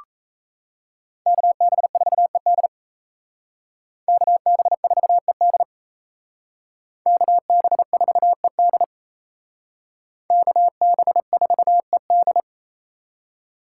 Morse code, heard as a keyed tone, about 37 wpm, 700 Hz. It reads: KB4ED KB4ED KB4ED KB4ED